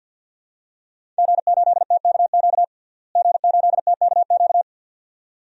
Morse code, heard as a keyed tone, about 47 wpm, 700 Hz. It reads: G8TKQ G8TKQ
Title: DXCC-100 Call Signs - 5 Characters - Repeated 2x - 50wpm